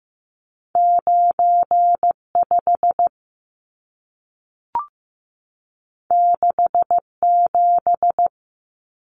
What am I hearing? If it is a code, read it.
Morse code, heard as a keyed tone, about 15 wpm, 700 Hz. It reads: 95 67
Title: Numbers - 2-Digit 15wpm